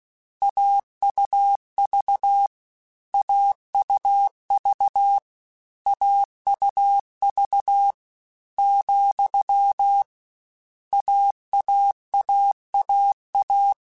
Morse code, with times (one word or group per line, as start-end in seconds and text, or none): 0.4-2.5 AUV
3.1-5.2 AUV
5.9-7.9 AUV
8.6-10.0 ,
10.9-13.7 AAAAA